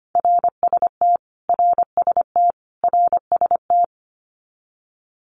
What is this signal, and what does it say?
Morse code, heard as a keyed tone, about 25 wpm, 700 Hz. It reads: RST RST RST
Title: Abbreviations Repeated 3x - 25wpm